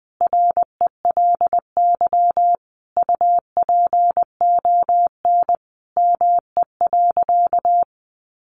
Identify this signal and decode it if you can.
Morse code, heard as a keyed tone, about 20 wpm, 700 Hz. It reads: RELY UPON ME.
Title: Sentences from Top 1200 Words 20wpm